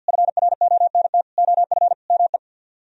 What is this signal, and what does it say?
Morse code, heard as a keyed tone, about 64 wpm, 700 Hz. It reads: FRONT OF ME